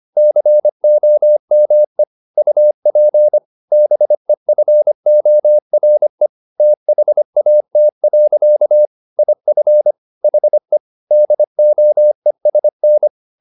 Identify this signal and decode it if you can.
Morse code, heard as a keyed tone, about 25 wpm, 600 Hz. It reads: COME UP BEFORE THAT. IF HE DOESN
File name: the old man and the sea part 1 25wpm